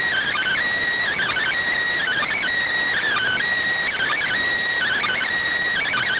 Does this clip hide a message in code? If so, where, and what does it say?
Yes: Emergency Alert System (SAME) tones from 1.1 to 5.3 s, heard as harsh buzzing bursts.